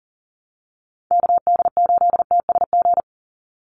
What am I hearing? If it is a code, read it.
Morse code, heard as a keyed tone, about 40 wpm, 700 Hz. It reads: KD8TSG
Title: USA Call Signs 40wpm